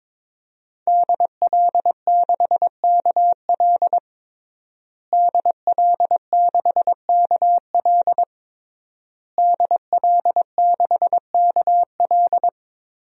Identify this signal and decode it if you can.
Morse code, heard as a keyed tone, about 21 wpm, 700 Hz. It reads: DL6KL DL6KL DL6KL
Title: DXCC-100 Call Signs - 5 Characters - Repeated 3x - 22wpm